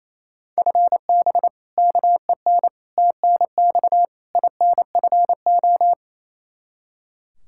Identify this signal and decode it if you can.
Morse code, heard as a keyed tone, about 28 wpm, 700 Hz. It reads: FB KEN TNX INFO
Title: Practice QSO - Exchange 3 - More Info - 28wpm